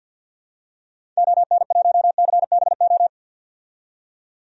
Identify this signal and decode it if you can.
Morse code, heard as a keyed tone, about 52 wpm, 700 Hz. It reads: ON1CBO
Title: DXCC-100 Call Signs - 6 Characters - 50wpm